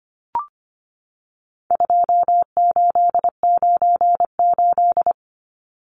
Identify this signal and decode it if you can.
Morse code, heard as a keyed tone, about 25 wpm, 700 Hz. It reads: 2898
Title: Numbers - 4-Digit 25wpm